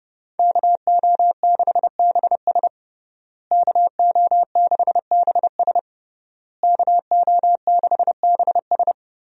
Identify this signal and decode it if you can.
Morse code, heard as a keyed tone, about 30 wpm, 700 Hz. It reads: KO6BS KO6BS KO6BS
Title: USA Call Signs - 5 Characters - Repeated 3x - 30wpm